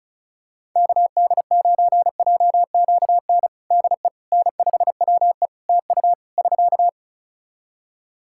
Morse code, heard as a keyed tone, about 36 wpm, 700 Hz. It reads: KD9JQN DE N5WE TU <SK>
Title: Practice QSO - Exchange 4 - Ending - 35wpm